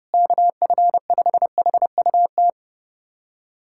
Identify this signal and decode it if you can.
Morse code, heard as a keyed tone, about 30 wpm, 700 Hz. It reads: KF5HUT
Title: USA Callsign DE Callsign - 6 Characters - 30wpm